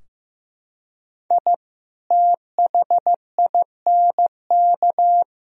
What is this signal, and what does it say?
Morse code, heard as a keyed tone, about 16 wpm, 700 Hz. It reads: I THINK
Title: Sentences from Top 1000 Words - Review Part 4 - 15wpm